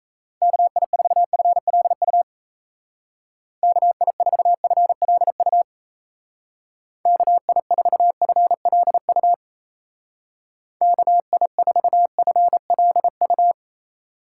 Morse code, heard as a keyed tone, about 35 wpm, 700 Hz. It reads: KI4FLU KI4FLU KI4FLU KI4FLU
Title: Speed Racing - USA Call Signs 28wpm